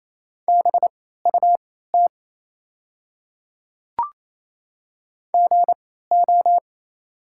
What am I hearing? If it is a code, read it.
Morse code, heard as a keyed tone, about 28 wpm, spaced as for 16 wpm, 690 Hz. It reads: BUT GO
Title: 3x Character Spacing - Top 100 Words - 28wpm